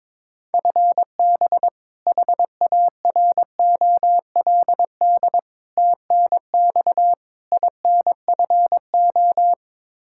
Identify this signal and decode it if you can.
Morse code, heard as a keyed tone, about 22 wpm, 700 Hz. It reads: FB HAROLD TNX INFO
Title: Practice QSO - Exchange 3 - More Info - 22wpm